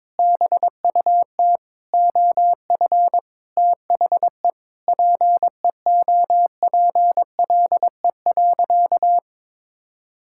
Morse code, heard as a keyed tone, about 22 wpm, 700 Hz. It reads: BUT OF THE PEOPLE.
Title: Sentences from Top 1600 Words 22wpm